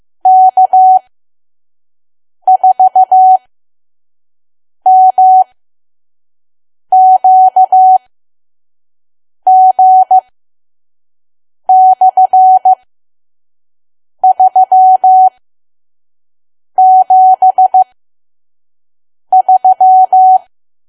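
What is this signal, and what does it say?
Morse code, heard as a keyed tone, about 15 wpm, spaced as for 5 wpm, 750 Hz. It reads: K4MQG/373
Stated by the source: Morse code practice at 5 WPM from KQ2P.COM.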